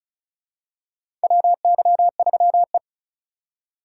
Morse code, heard as a keyed tone, about 34 wpm, 700 Hz. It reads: WY3E
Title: USA Call Signs - 4 Characters - 3x Character Spacing - 35wpm